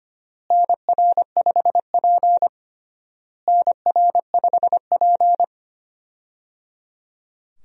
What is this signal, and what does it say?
Morse code, heard as a keyed tone, about 25 wpm, 700 Hz. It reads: NR5P NR5P
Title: USA Call Signs - 4 Characters - Repeated 3x - 25wpm